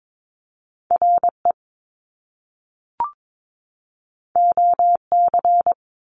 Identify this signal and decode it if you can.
Morse code, heard as a keyed tone, about 22 wpm, 700 Hz. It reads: R E O C
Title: Top Two Letter Combinations in Words 22wpm